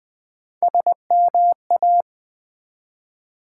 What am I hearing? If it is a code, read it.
Morse code, heard as a keyed tone, about 20 wpm, 700 Hz. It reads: SMA
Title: Build A Word - 1 Letter At A Time - 5-Letter Words - 20wpm